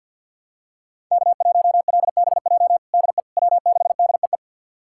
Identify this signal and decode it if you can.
Morse code, heard as a keyed tone, about 52 wpm, 700 Hz. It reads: K1LBJ DE W6DEE
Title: USA Callsign DE Callsign - 5 Characters - 50wpm